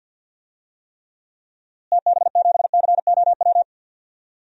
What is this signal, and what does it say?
Morse code, heard as a keyed tone, about 52 wpm, 700 Hz. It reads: TB7COW
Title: DXCC-100 Call Signs - 6 Characters - Repeated 2x - 50wpm